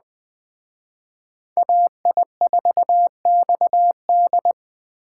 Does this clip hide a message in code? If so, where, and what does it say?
Yes: Morse code from 1.6 to 4.5 s, heard as a keyed tone, about 19 wpm, 700 Hz. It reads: AI4XD